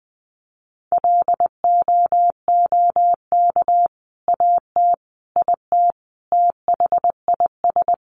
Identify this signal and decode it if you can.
Morse code, heard as a keyed tone, about 20 wpm, 700 Hz. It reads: LOOK AT IT THIS
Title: Sentences from Top 100 Words 20wpm